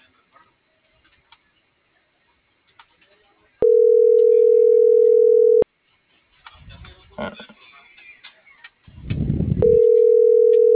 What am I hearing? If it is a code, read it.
Ringback tone (North America), 440 + 480 Hz.